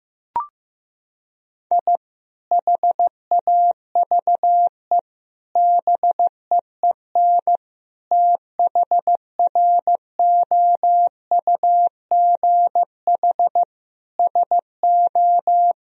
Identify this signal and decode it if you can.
Morse code, heard as a keyed tone, about 15 wpm, 700 Hz. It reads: I HAVE BEEN THROUGH SO
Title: Sentences from Top 1000 Words - Review Part 4 - 15wpm